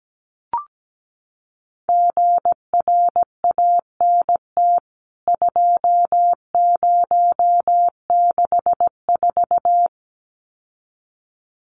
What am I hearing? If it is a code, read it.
Morse code, heard as a keyed tone, about 17 wpm, 700 Hz. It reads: GRANT 2064
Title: CWT Exchange 17wpm